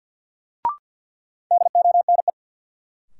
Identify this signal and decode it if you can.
Morse code, heard as a keyed tone, about 50 wpm, 700 Hz. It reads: DONE
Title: Sets of 5 Words Encore - New Word Drill - 50wpm